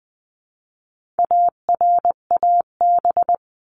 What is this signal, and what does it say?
Morse code, heard as a keyed tone, about 20 wpm, 700 Hz. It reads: ARAB
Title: 259 Lesson 42d - Words 18-20wpm